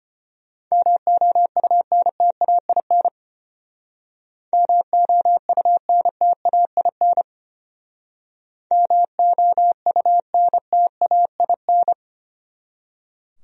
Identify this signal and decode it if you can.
Morse code, heard as a keyed tone, about 33 wpm, 700 Hz. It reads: MOUNTAIN MOUNTAIN MOUNTAIN